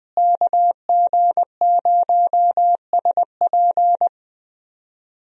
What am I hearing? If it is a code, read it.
Morse code, heard as a keyed tone, about 21 wpm, 700 Hz. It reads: KG0SP
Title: USA Call Signs - 5 Characters - Repeated 3x - 20wpm